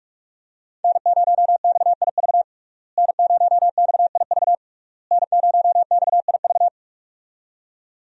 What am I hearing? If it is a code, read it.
Morse code, heard as a keyed tone, about 46 wpm, 700 Hz. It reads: N0XIV N0XIV N0XIV
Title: USA Call Signs - 5 Characters - Repeated 3x - 45wpm